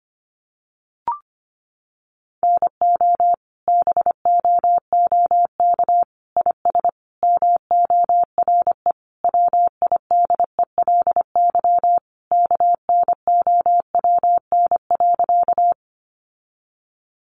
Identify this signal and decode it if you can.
Morse code, heard as a keyed tone, about 25 wpm, 700 Hz. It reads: NO BOOK IS MORE WIDELY KNOWN.
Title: Sentences from Top 1200 Words 25wpm